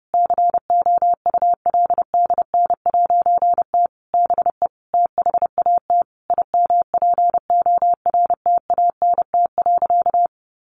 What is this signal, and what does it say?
Morse code, heard as a keyed tone, about 30 wpm, 700 Hz. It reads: COULDN'T BE THAT IMPORTANT.
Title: A Sound Of Thunder 30wpm